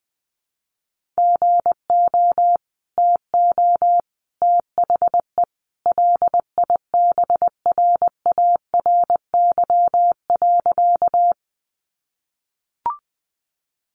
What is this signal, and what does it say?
Morse code, heard as a keyed tone, about 20 wpm, 700 Hz. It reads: GO TO THE LIBRARY.
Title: Sentences from Top 600 Words 20wpm